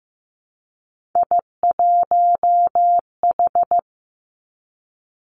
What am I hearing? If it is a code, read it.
Morse code, heard as a keyed tone, about 15 wpm, 700 Hz. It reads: I1H
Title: Three Challenging Letter-Number Combinations 15wpm